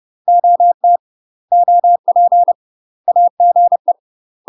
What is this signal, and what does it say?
Morse code, heard as a keyed tone, about 30 wpm, 700 Hz. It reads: OT OP AGE
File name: UK MASTER QSOs TRAINING / 30WPM / UK MASTER QSOs 0021